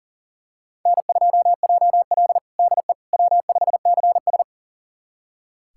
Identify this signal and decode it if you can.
Morse code, heard as a keyed tone, about 40 wpm, 700 Hz. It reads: N2JL DE W5CS
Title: USA Callsign DE Callsign - 4 Characters - 40wpm